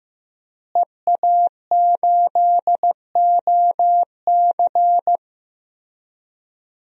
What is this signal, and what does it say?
Morse code, heard as a keyed tone, about 15 wpm, 700 Hz. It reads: EA8OC
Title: DXCC-100 Call Signs - 5 Characters - Repeated 3x - 15wpm